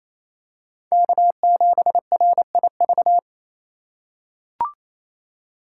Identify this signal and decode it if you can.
Morse code, heard as a keyed tone, about 28 wpm, 700 Hz. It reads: K7RIV
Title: USA Call Signs - 5 Characters - 2x Character Spacing - 28wpm